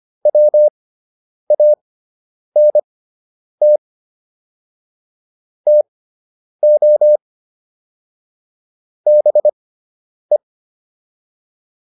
Morse code, heard as a keyed tone, about 25 wpm, spaced as for 9 wpm, 600 Hz. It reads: WANT TO BE